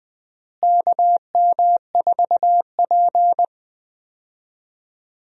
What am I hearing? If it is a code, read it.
Morse code, heard as a keyed tone, about 21 wpm, 700 Hz. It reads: KM4P
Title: USA Call Signs - 4 Characters - Repeated 3x - 20wpm